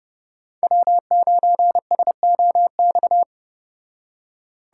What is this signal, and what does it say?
Morse code, heard as a keyed tone, about 31 wpm, 700 Hz. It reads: W9SOX